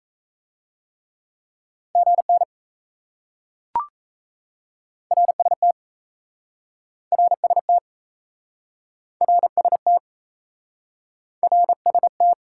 Morse code, heard as a keyed tone, about 38 wpm, 700 Hz. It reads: GN RST RST RST RST